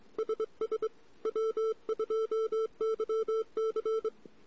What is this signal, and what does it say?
Morse code, heard as a keyed tone, about 23 wpm, 440 Hz. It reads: SS W2YC
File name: CW / 2008-ss-cw-k5tr-op-n5rz -14:00